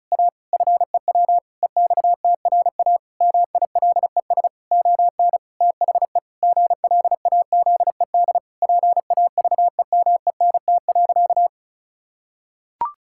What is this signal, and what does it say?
Morse code, heard as a keyed tone, about 34 wpm, 700 Hz. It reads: A FEW EXTRA MILES ON THE GLAZED PAVEMENT.
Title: Pangram Challenge v2 - 35wpm